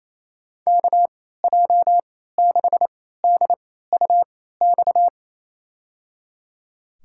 Morse code, heard as a keyed tone, about 28 wpm, spaced as for 16 wpm, 700 Hz. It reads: KJ6DUX
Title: USA Call Signs - 6 Characters - 3x Character Spacing - 28wpm